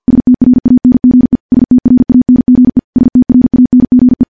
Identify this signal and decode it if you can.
Morse code, heard as a keyed tone, about 50 wpm, 250 Hz. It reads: STRANGE STRANGE STRANGE
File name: morse / morse-training-50WPM